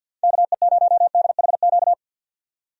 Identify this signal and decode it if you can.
Morse code, heard as a keyed tone, about 51 wpm, 700 Hz. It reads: KE0DHQ
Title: USA Call Signs - 6 Characters - 50wpm